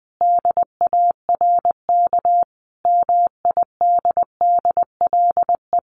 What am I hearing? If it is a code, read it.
Morse code, heard as a keyed tone, about 20 wpm, 700 Hz. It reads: DARK MIDDLE
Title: Sentences from Top 900 Words 20wpm